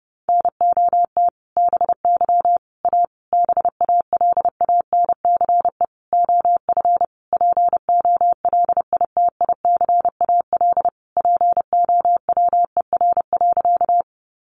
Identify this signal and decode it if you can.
Morse code, heard as a keyed tone, about 30 wpm, 700 Hz. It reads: NOT BY A BALANCE OF POLITICAL POWER.
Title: Sentences from Top 900 Words 30wpm